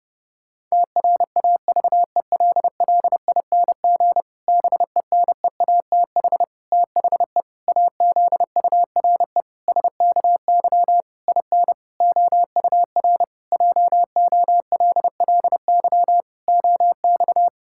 Morse code, heard as a keyed tone, about 30 wpm, 700 Hz. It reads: TRAVELLING BENEATH THE AZURE SKY IN OUR JOLLY OX
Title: Pangram Challenge v2 - 30wpm